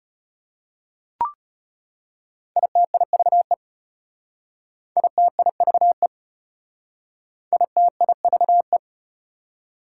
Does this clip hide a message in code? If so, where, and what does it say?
Yes: Morse code from 2.6 to 8.8 s, heard as a keyed tone, about 34 wpm, 700 Hz. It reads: ITIVE ITIVE ITIVE